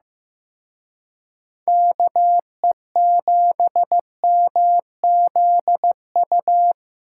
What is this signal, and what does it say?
Morse code, heard as a keyed tone, about 18 wpm, 700 Hz. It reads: KE7MZU
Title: Speed Racing - USA 2x3 Call Signs 15wpm